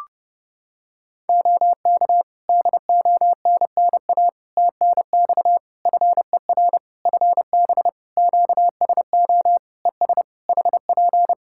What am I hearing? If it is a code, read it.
Morse code, heard as a keyed tone, about 29 wpm, 700 Hz. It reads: OK DONNA TNX FER FB QSO ES HP
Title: Practice QSO - Exchange 4 - Ending - 30wpm